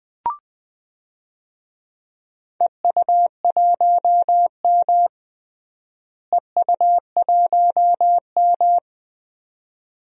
Morse code, heard as a keyed tone, about 20 wpm, 700 Hz. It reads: EU1M EU1M
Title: DXCC-100 Call Signs - 4 Characters - Repeated 2x - 20wpm